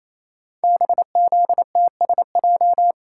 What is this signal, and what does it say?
Morse code, heard as a keyed tone, about 27 wpm, 700 Hz. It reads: BZTSJ
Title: Code Groups - 5 Letters - 28wpm